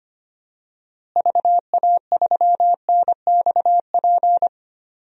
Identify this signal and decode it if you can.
Morse code, heard as a keyed tone, about 25 wpm, 700 Hz. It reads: VA3NXP
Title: DXCC-100 Call Signs - 6 Characters - Repeated 2x - 25wpm